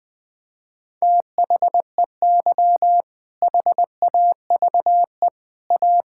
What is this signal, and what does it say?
Morse code, heard as a keyed tone, about 20 wpm, 700 Hz. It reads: THEY HAVE A
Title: Sentences from Top 1000 Words - Review Part 5 - 20wpm